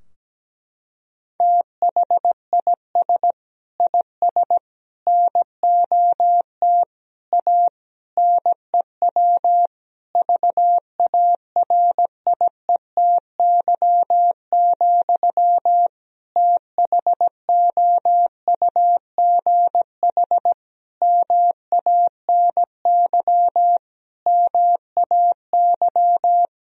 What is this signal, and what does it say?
Morse code, heard as a keyed tone, about 17 wpm, 700 Hz. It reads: THIS IS NOT A NEW VARIETY, THOUGH MANY MAY
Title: Sentences from Top 1000 Words - Review Part 4 - 17wpm